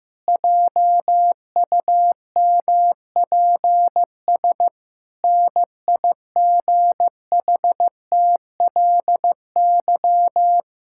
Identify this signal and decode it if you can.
Morse code, heard as a keyed tone, about 16 wpm, 700 Hz. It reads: JUMPS NIGHTLY
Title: Pangram Challenge v2 - 15wpm